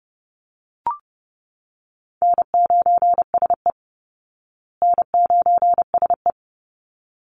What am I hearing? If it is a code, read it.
Morse code, heard as a keyed tone, about 30 wpm, 700 Hz. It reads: N9SE N9SE
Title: Easy Call Signs Repeated 2x - 30wpm